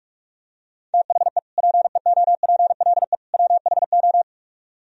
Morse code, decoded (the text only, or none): THE PEOPLE WHO